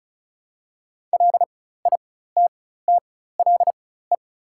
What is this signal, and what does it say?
Morse code, heard as a keyed tone, about 36 wpm, spaced as for 16 wpm, 690 Hz. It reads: LITTLE
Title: 4x Character Spacing - Top 300 Words - 35wpm